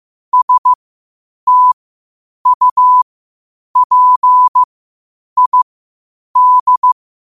Morse code, heard as a keyed tone, about 15 wpm, spaced as for 8 wpm, 1000 Hz. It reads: STUPID
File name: qotdfiles / 8wpm